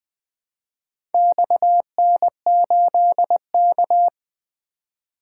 Morse code, heard as a keyed tone, about 20 wpm, 700 Hz. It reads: XN8K
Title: Four Letter-Number Combinations - 20wpm